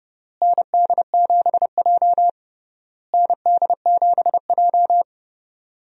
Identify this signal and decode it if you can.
Morse code, heard as a keyed tone, about 30 wpm, 700 Hz. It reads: ND7J ND7J
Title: USA Call Signs - 4 Characters - Repeated 3x - 30wpm